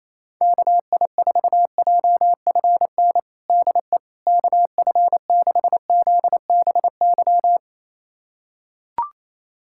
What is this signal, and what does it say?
Morse code, heard as a keyed tone, about 28 wpm, 700 Hz. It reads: KI4JFN DE KF6ZBY
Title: USA Callsign DE Callsign - 6 Characters - 28wpm